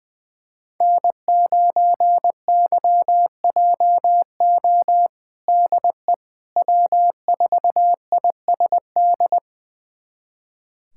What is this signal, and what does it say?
Morse code, heard as a keyed tone, about 20 wpm, 700 Hz. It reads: N9YJO DE W4ISD
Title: USA Callsign DE Callsign - 5 Characters - 20wpm